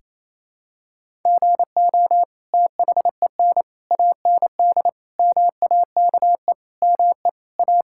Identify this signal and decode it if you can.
Morse code, heard as a keyed tone, about 28 wpm, 700 Hz. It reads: GO THEN AND MAKE ME A
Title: Sentences from Top 1000 Words - Review Part 5 - 28wpm